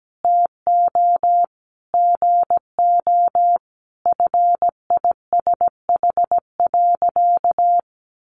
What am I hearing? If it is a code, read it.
Morse code, heard as a keyed tone, about 17 wpm, 700 Hz. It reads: TO GO FISH.
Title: Sentences from Top 1400 Words 17wpm